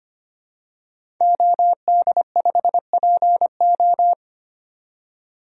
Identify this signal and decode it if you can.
Morse code, heard as a keyed tone, about 25 wpm, 700 Hz. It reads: OD5PO
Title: DXCC-100 Call Signs - 5 Characters - 25wpm